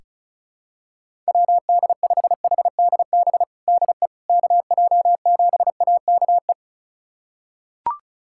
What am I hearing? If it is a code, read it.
Morse code, heard as a keyed tone, about 36 wpm, 700 Hz. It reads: WD5HDB DE KJ7AKE
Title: USA Callsign DE Callsign - 6 Characters - 35wpm